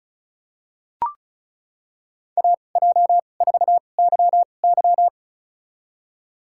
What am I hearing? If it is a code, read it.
Morse code, heard as a keyed tone, about 36 wpm, spaced as for 25 wpm, 700 Hz. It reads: AJ4YY
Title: USA Call Signs - 5 Characters - 2x Character Spacing - 35wpm